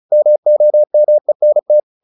Morse code, heard as a keyed tone, about 34 wpm, 600 Hz. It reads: MOMENT
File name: the old man and the sea / the old man and the sea 1 35wpm